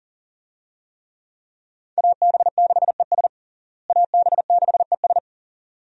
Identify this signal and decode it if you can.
Morse code, heard as a keyed tone, about 40 wpm, 700 Hz. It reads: AB6ES AB6ES
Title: USA Call Signs - 5 Characters - Repeated 2x - 40wpm